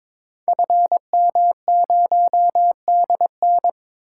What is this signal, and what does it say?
Morse code, heard as a keyed tone, about 22 wpm, 700 Hz. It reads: FM0DN
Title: Code Groups - 5 Letter:Numbers - 22wpm